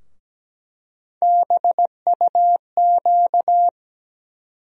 Morse code, heard as a keyed tone, about 17 wpm, 700 Hz. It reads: BUQ
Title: USA Call Sign Postfixes - 3 Letters 17wpm